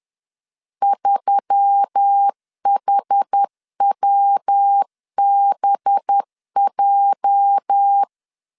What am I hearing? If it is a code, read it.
Morse code, heard as a keyed tone, about 10 wpm, 800 Hz. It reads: 3HWBJ
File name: LEZIONI / 8 YJ45 / 8 YJ 10wpm 2651